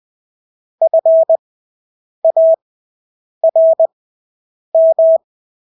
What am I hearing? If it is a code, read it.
Morse code, heard as a keyed tone, about 20 wpm, spaced as for 8 wpm, 650 Hz. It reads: FARM
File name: Engelske ord 20/8 wpm